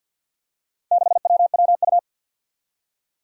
Morse code, heard as a keyed tone, about 49 wpm, 700 Hz. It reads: 6WWU
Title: Four Challenging Letter-Number Combinations 50wpm